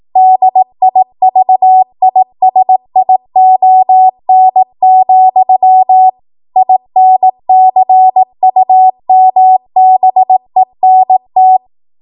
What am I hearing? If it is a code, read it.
Morse code, heard as a keyed tone, about 18 wpm, 750 Hz. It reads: DIVISION, INCUMBENT